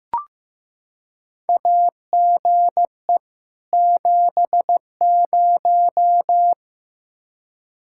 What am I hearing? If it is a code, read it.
Morse code, heard as a keyed tone, about 15 wpm, 700 Hz. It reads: AGE 70
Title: Practice QSO - Exchange 3 - More Info - 15wpm